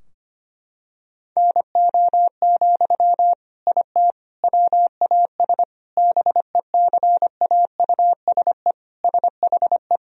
Morse code, heard as a keyed tone, about 25 wpm, 700 Hz. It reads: NO, IT WAS BECAUSE SHE
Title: Sentences from Top 1000 Words - Review Part 4 - 25wpm